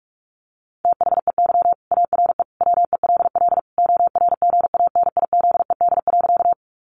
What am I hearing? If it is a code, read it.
Morse code, heard as a keyed tone, about 44 wpm, 700 Hz. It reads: THEY ARE WELL ORGANIZED.
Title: Sentences from Top 1300 Words 45wpm